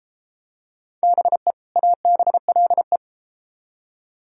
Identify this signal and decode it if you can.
Morse code, heard as a keyed tone, about 32 wpm, 700 Hz. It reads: BE ABLE